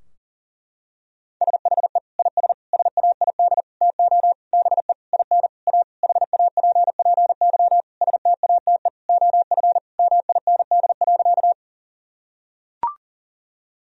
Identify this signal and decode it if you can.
Morse code, heard as a keyed tone, about 40 wpm, 700 Hz. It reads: SHE IS SAID TO BE IN A HAPPY STATE OF MIND.
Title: Sentences from Top 1000 Words - Review Part 2 - 40wpm